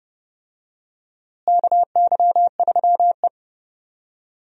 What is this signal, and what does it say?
Morse code, heard as a keyed tone, about 31 wpm, 700 Hz. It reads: KY3E
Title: USA Call Signs - 4 Characters - Repeated 2x - 30wpm